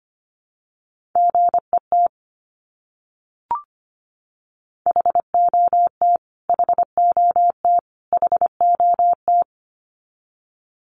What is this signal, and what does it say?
Morse code, heard as a keyed tone, about 25 wpm, 700 Hz. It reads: GET HOT HOT HOT
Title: Top 200 Words Repeated 25wpm